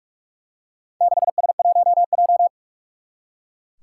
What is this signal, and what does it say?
Morse code, heard as a keyed tone, about 44 wpm, 700 Hz. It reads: 6S1J
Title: Four Challenging Letter-Number Combinations 45wpm